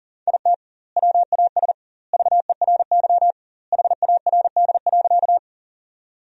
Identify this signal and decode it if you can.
Morse code, heard as a keyed tone, about 40 wpm, 700 Hz. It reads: IT WAS VERY HARD.
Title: 2x Word-Spacing - Sentences from Top 300 Words - 40wpm